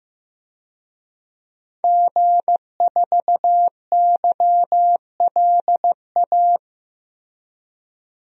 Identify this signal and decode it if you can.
Morse code, heard as a keyed tone, about 16 wpm, 700 Hz. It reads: G4YLA
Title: DXCC-100 Call Signs - 5 Characters - Repeated 3x - 15wpm